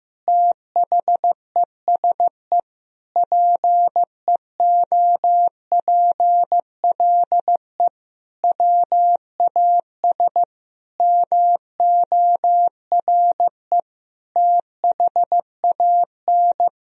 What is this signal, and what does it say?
Morse code, heard as a keyed tone, about 16 wpm, 700 Hz. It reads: THESE PEOPLE WAS MORE THAN
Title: Sentences from Top 1000 Words - Review Part 2 - 15wpm